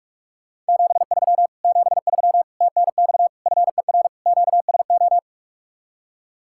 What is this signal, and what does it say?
Morse code, heard as a keyed tone, about 60 wpm, spaced as for 52 wpm, 700 Hz. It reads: 73 73 TNX FER QSO
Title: Speed-Racing Common QSO Phrases - 45wpm